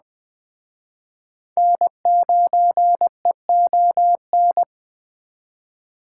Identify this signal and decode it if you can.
Morse code, heard as a keyed tone, about 19 wpm, 700 Hz. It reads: N9EON